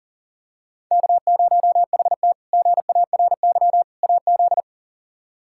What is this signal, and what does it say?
Morse code, heard as a keyed tone, about 40 wpm, 700 Hz. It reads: K0HT GARY AZ
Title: K1USN SST QSO Practice - 40wpm